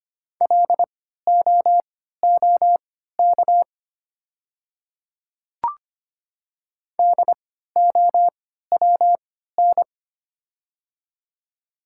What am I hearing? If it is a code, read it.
Morse code, heard as a keyed tone, about 25 wpm, spaced as for 14 wpm, 700 Hz. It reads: LOOK DOWN
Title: 3x Character Spacing - Top 100 Words - 25wpm